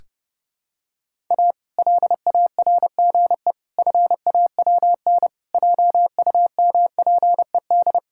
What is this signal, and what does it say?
Morse code, heard as a keyed tone, about 30 wpm, 700 Hz. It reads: A LARGE FAWN JUMPED
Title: Pangram Challenge v2 - 30wpm